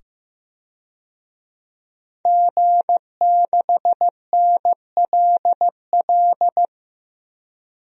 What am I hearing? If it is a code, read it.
Morse code, heard as a keyed tone, about 15 wpm, 700 Hz. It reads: G6NLL
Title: DXCC-100 Call Signs - 5 Characters - 15wpm